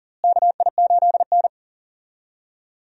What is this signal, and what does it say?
Morse code, heard as a keyed tone, about 40 wpm, 700 Hz. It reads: KI8N
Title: USA Call Signs - 4 Characters - Repeated 3x - 40wpm